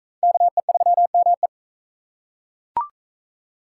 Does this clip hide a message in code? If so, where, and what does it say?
Yes: Morse code from 0.2 to 1.5 s, heard as a keyed tone, about 40 wpm, 700 Hz. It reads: KE3ME